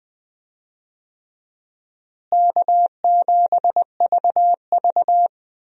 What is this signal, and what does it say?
Morse code, heard as a keyed tone, about 20 wpm, 700 Hz. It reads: K7VV
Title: USA Call Signs - 4 Characters - Repeated 3x - 20wpm